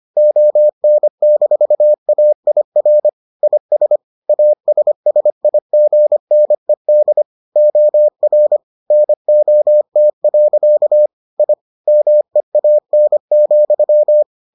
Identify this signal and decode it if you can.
Morse code, heard as a keyed tone, about 25 wpm, 600 Hz. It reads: ON-AIR IS ASSIGNED OR NOT. I MEAN,